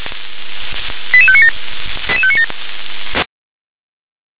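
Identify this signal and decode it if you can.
Five-tone selective call (ZVEI), heard as quick runs of five tones, tones of about 70 ms. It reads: 88597 88597
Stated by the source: Received by SDR, ZVEI five-tone call.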